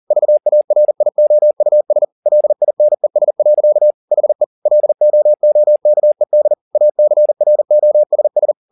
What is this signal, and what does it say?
Morse code, heard as a keyed tone, about 40 wpm, 600 Hz. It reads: VARIOUS LINES. HE LOOKED ACROSS